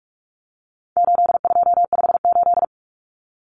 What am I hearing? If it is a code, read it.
Morse code, heard as a keyed tone, about 44 wpm, 700 Hz. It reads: 8258
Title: Common Cut Numbers - 4-Digit 45wpm